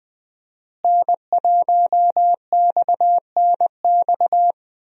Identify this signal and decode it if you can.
Morse code, heard as a keyed tone, about 21 wpm, 700 Hz. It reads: N1XNX
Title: USA Call Signs - 5 Characters - Repeated 3x - 20wpm